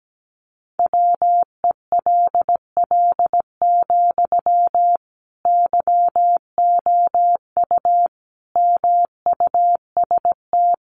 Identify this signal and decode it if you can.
Morse code, heard as a keyed tone, about 17 wpm, 700 Hz. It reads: WELL, YOU MUST
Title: Sentences from Top 700 Words 17wpm